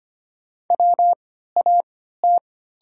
Morse code, heard as a keyed tone, about 25 wpm, spaced as for 14 wpm, 690 Hz. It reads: WAT
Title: 3x Character Spacing - Top 300 Words - 25wpm